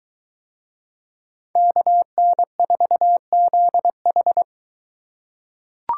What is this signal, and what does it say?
Morse code, heard as a keyed tone, about 22 wpm, 700 Hz. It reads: KN4ZH